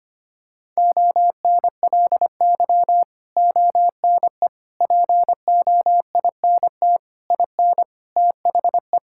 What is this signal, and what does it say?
Morse code, heard as a keyed tone, about 25 wpm, 700 Hz. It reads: ONLY ONE POINT IN THE
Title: Sentences from Top 1000 Words - Review Part 5 - 25wpm